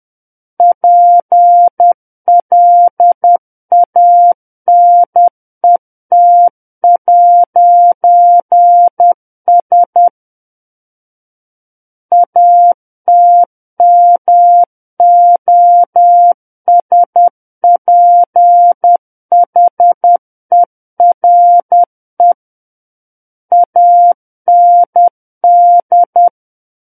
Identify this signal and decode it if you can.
Morse code, heard as a keyed tone, about 10 wpm, 700 Hz. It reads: PLANET'S ATMOSPHERE AND
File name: Book One / 01-Chapter 1 - 10 WPM